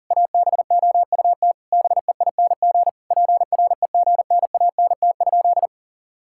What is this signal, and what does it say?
Morse code, heard as a keyed tone, about 37 wpm, 700 Hz. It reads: ABOUT BEING PREGNANT?
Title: Sentences from Top 1000 Words - Review Part 6 - 40wpm